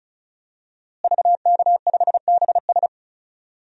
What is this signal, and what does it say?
Morse code, heard as a keyed tone, about 36 wpm, 700 Hz. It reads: VK5BS
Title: DXCC-100 Call Signs - 5 Characters - Repeated 2x - 35wpm